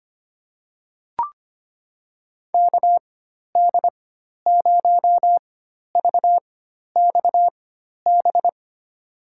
Morse code, heard as a keyed tone, about 25 wpm, spaced as for 12 wpm, 700 Hz. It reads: KD0VXB
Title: USA Call Signs - 6 Characters - 4x Character Spacing - 25wpm